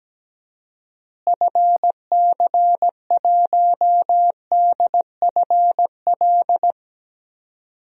Morse code, heard as a keyed tone, about 17 wpm, 700 Hz. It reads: FC1DFL
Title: DXCC-100 Call Signs - 6 Characters - Repeated 2x - 17wpm